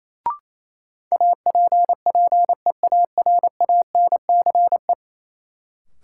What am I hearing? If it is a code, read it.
Morse code, heard as a keyed tone, about 28 wpm, 700 Hz. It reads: APPEARANCE
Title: Sentences from Top 1100 Words - New Word Drill - 28wpm